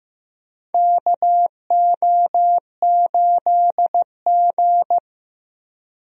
Morse code, heard as a keyed tone, about 15 wpm, 700 Hz. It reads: KO8G
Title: USA Call Signs - 4 Characters - Repeated 2x - 15wpm